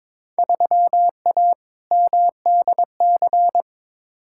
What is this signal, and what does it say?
Morse code, heard as a keyed tone, about 22 wpm, 700 Hz. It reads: 3A MDC
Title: Field Day Exchanges - 22wpm